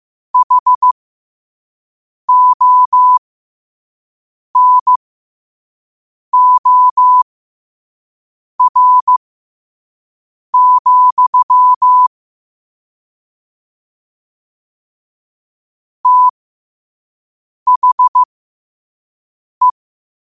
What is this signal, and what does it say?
Morse code, heard as a keyed tone, about 15 wpm, spaced as for 5 wpm, 1000 Hz. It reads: HONOR, THE
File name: qotdfiles / 5wpm